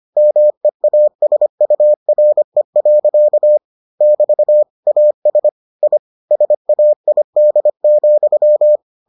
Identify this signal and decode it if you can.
Morse code, heard as a keyed tone, about 25 wpm, 600 Hz. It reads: MEASURE. <BT> AS I SAID,